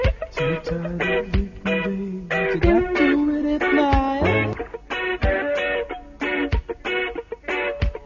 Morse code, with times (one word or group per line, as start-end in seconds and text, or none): none